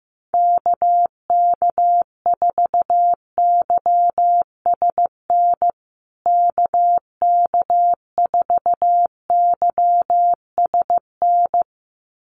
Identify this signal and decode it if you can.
Morse code, heard as a keyed tone, about 15 wpm, 700 Hz. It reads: KK4YSN KK4YSN
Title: USA Call Signs Repeated 2x 15wpm